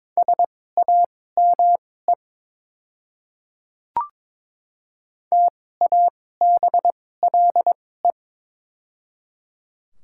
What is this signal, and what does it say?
Morse code, heard as a keyed tone, about 22 wpm, spaced as for 16 wpm, 690 Hz. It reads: SAME TABLE
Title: 2x Character Spacing - Top 500 Words - 22wpm